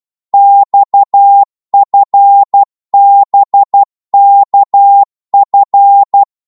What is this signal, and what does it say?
Morse code, heard as a keyed tone, about 12 wpm, 800 Hz. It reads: XFBKF
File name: Morse / ltr-12wpm